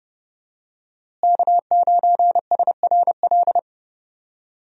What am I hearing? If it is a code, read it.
Morse code, heard as a keyed tone, about 30 wpm, 700 Hz. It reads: K9SRL
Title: USA Call Signs - 5 Characters - Repeated 2x - 30wpm